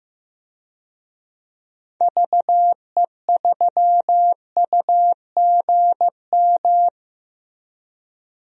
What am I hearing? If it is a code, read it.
Morse code, heard as a keyed tone, about 15 wpm, 700 Hz. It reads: VE3UGM
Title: DXCC-100 Call Signs - 6 Characters - Repeated 2x - 15wpm